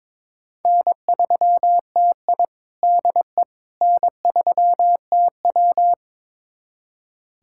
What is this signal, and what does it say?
Morse code, heard as a keyed tone, about 23 wpm, 700 Hz. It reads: N3TI DE N3TW
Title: USA Callsign DE Callsign - 4 Characters - 22wpm